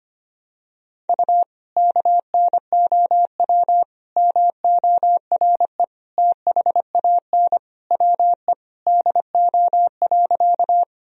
Morse code, heard as a keyed tone, about 25 wpm, 700 Hz. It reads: U KNOW MORE THAN WE DO.
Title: Sentences from Top 100 Words and Abbreviations 25wpm